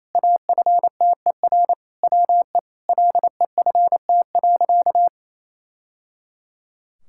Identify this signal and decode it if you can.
Morse code, heard as a keyed tone, about 27 wpm, 700 Hz. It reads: AFTER WE LEFT.
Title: Sentences from Top 1000 Words - Review Part 6 - 28wpm